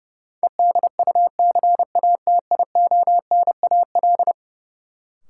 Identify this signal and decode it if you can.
Morse code, heard as a keyed tone, about 31 wpm, 700 Hz. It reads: EDUCATIONAL
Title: Sentences from Top 1200 Words - New Word Drill Alt - 30wpm